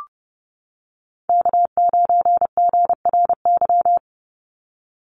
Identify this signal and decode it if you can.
Morse code, heard as a keyed tone, about 30 wpm, 700 Hz. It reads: K9GRY
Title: USA Call Signs 30wpm